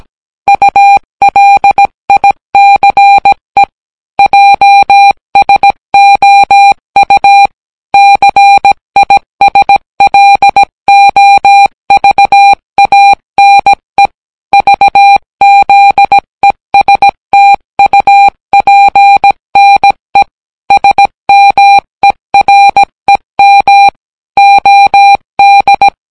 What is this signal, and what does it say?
Morse code, heard as a keyed tone, about 17 wpm, 800 Hz. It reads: ULICE JSOU CISLOVANE VZESTUPNE SMEREM OD